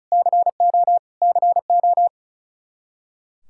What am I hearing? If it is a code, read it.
Morse code, heard as a keyed tone, about 36 wpm, 700 Hz. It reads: CO CO
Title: K1USN SST QSO Practice 35wpm